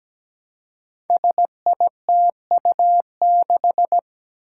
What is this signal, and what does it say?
Morse code, heard as a keyed tone, about 17 wpm, 700 Hz. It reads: SITU6
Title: Five Challenging Letter-Number Combinations 17wpm